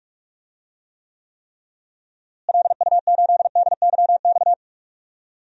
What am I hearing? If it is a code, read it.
Morse code, heard as a keyed tone, about 41 wpm, 700 Hz. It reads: LU8DYX
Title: DXCC-100 Call Signs - 6 Characters - Repeated 3x - 45wpm